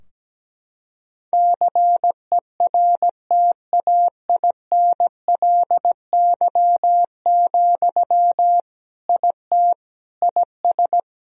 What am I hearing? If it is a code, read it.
Morse code, heard as a keyed tone, about 17 wpm, 700 Hz. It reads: CERTAINLY, IT IS
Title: Sentences from Top 1000 Words - Review Part 2 - 17wpm